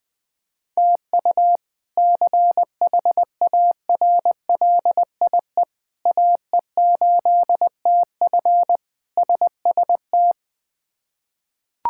Morse code, heard as a keyed tone, about 20 wpm, 700 Hz. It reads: TU CHARLIE AE8TF SST
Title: K1USN SST QSO Practice - 20wpm